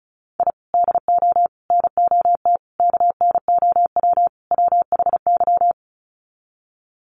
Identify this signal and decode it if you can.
Morse code, heard as a keyed tone, about 34 wpm, 700 Hz. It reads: I DO NOT KNOW WHY
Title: Sets of 5 Words 35wpm